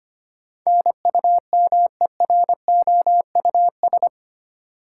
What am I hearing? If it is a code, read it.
Morse code, heard as a keyed tone, about 25 wpm, 700 Hz. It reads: NUMEROUS
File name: 8 Letter Words - First 3 Letters Given - 25wpm